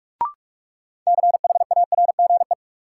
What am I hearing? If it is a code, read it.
Morse code, heard as a keyed tone, about 43 wpm, 700 Hz. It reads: CHARGE
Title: Sentences from Top 900 Words - New Word Drill - 45wpm